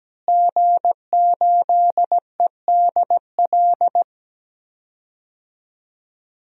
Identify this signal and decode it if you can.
Morse code, heard as a keyed tone, about 17 wpm, 700 Hz. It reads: G8EDL
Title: DXCC-100 Call Signs - 5 Characters - Repeated 2x - 17wpm